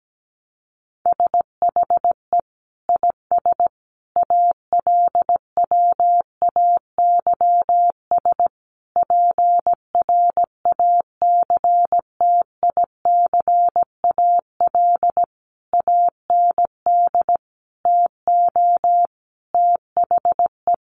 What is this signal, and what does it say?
Morse code, heard as a keyed tone, about 17 wpm, 700 Hz. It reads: SHE IS ALWAYS PRACTICAL AND TO THE
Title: Sentences from Top 1400 Words 17wpm